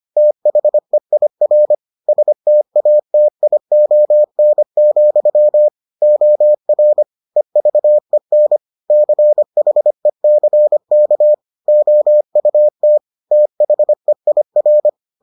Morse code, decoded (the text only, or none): THEIR STATION, OR EVEN CHECK OUT THEIR